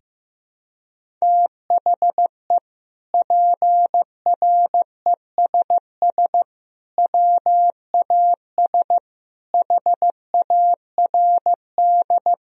Morse code, heard as a keyed tone, about 15 wpm, 700 Hz. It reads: THE PRESS WAS HARD
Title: Sentences from Top 1000 Words - Review Part 1 - 15wpm